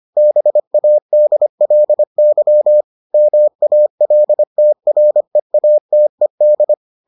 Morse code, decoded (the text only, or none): BADLY MALTREATED